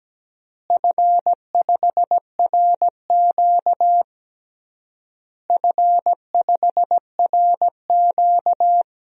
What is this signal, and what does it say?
Morse code, heard as a keyed tone, about 17 wpm, 700 Hz. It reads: F5RQ F5RQ
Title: DXCC-100 Call Signs - 4 Characters - Repeated 3x - 17wpm